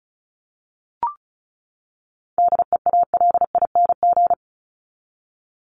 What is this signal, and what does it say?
Morse code, heard as a keyed tone, about 34 wpm, 700 Hz. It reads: DEALING
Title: ING Suffix - Top 100 Words with Base Words 35wpm